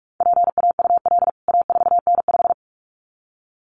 Morse code, heard as a keyed tone, about 44 wpm, 700 Hz. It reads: PAUL A4N5
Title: CWT Exchange 45wpm